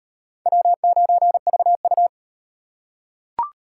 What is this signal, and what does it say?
Morse code, heard as a keyed tone, about 38 wpm, 700 Hz. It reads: W9VU